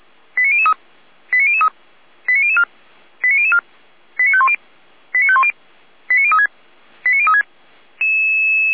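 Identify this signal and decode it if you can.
Five-tone selective call (ZVEI), heard as quick runs of five tones, tones of about 70 ms. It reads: 89003 89003 89004 89004 89510 89510 89036 89036
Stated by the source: ZVEI five-tone call 89010 received by SDR, recorded by feuerwehr-orlamuende.org fire brigade server.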